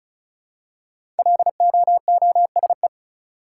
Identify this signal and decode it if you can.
Morse code, heard as a keyed tone, about 34 wpm, 700 Hz. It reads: LOOSE
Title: Five Letter Words - 35wpm